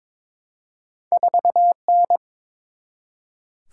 Morse code, heard as a keyed tone, about 22 wpm, 700 Hz. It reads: 4N
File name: practice / 22wpm / Two Letter-Number Combinations - 22wpm